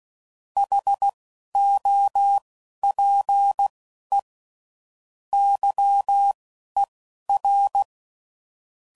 Morse code, heard as a keyed tone, about 16 wpm, spaced as for 11 wpm, 800 Hz. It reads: HOPE YER
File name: CW Code Groups @10WPM / QSOs / QSO WD0FBT